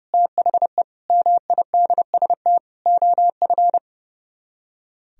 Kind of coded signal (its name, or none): Morse code